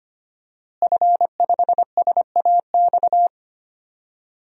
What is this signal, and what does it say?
Morse code, heard as a keyed tone, about 25 wpm, 700 Hz. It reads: F5SAX
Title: DXCC-100 Call Signs - 5 Characters - Repeated 3x - 25wpm